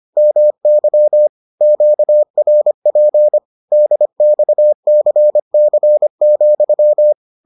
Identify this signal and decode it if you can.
Morse code, heard as a keyed tone, about 25 wpm, 600 Hz. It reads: MY QRP DXCC,